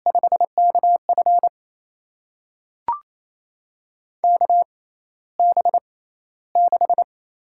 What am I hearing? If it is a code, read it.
Morse code, heard as a keyed tone, about 28 wpm, spaced as for 14 wpm, 700 Hz. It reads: 5KF KB6
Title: USA Call Signs - 6x Character Spacing - 28wpm